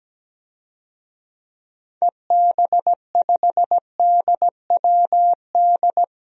Morse code, heard as a keyed tone, about 17 wpm, 700 Hz. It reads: EB5DWD
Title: DXCC-100 Call Signs - 6 Characters - Repeated 3x - 17wpm